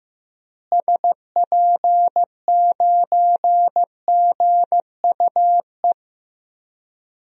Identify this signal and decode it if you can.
Morse code, heard as a keyed tone, about 15 wpm, 700 Hz. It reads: SP9GUE
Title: DXCC-100 Call Signs - 6 Characters - Repeated 3x - 15wpm